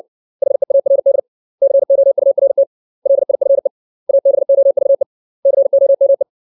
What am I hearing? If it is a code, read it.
Morse code, heard as a keyed tone, about 60 wpm, 550 Hz. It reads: HEARD COURT LIFE ABOVE COME
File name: 632 words / 632 Words 60WPM